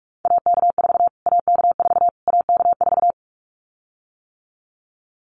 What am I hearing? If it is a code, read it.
Morse code, heard as a keyed tone, about 44 wpm, 700 Hz. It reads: AK4 AK4 AK4
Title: Callsign Prefixes for Top 25 DXCC - Repeated 3x - 45wpm